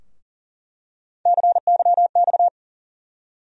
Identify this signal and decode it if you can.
Morse code, heard as a keyed tone, about 38 wpm, 700 Hz. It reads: CYX
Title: USA Call Sign Postfixes - 3 Letters 40wpm